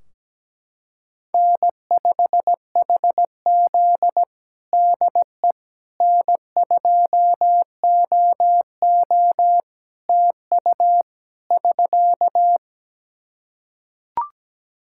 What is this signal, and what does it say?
Morse code, heard as a keyed tone, about 17 wpm, 700 Hz. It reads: N5HZ DE N2OO TU <SK>
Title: Practice QSO - Exchange 4 - Ending - 17wpm